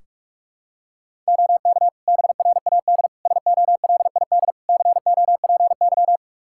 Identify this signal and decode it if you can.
Morse code, heard as a keyed tone, about 41 wpm, 700 Hz. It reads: OK BRAD SOLID COPY
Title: Practice QSO - Exchange 3 - More Info - 45wpm